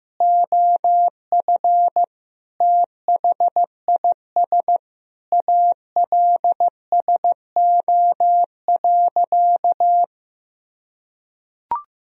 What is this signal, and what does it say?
Morse code, heard as a keyed tone, about 15 wpm, 700 Hz. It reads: OF THIS ALSO.
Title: Sentences from Top 1000 Words - Review Part 1 - 15wpm